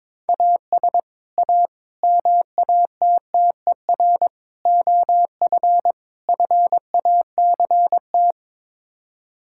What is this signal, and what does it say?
Morse code, heard as a keyed tone, about 22 wpm, 700 Hz. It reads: AS A MATTER OF FACT
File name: practice / 22wpm / Build A Phrase - 5-Word Phrases - 22wpm